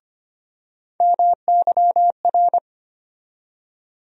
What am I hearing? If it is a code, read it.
Morse code, heard as a keyed tone, about 25 wpm, 700 Hz. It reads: MYR
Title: USA Call Sign Postfixes - 3 Letters 25wpm